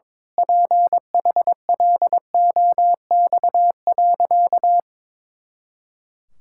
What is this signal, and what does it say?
Morse code, heard as a keyed tone, about 21 wpm, 700 Hz. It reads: PHLOX.
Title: Pangram Challenge v2 - 22wpm